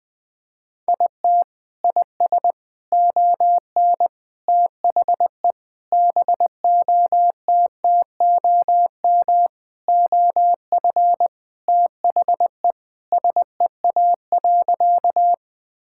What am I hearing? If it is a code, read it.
Morse code, heard as a keyed tone, about 21 wpm, 700 Hz. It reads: IT IS ON THE BOTTOM OF THE SEA.
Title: Sentences from Top 1000 Words - Review Part 6 - 20wpm